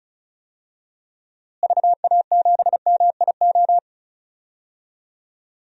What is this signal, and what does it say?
Morse code, heard as a keyed tone, about 36 wpm, 700 Hz. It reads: VA7MIO
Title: DXCC-100 Call Signs - 6 Characters - Repeated 3x - 35wpm